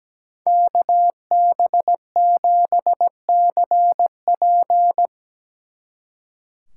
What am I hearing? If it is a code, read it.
Morse code, heard as a keyed tone, about 19 wpm, 700 Hz. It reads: KB7CP
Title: Speed Racing - USA 2x2 Call Signs 17wpm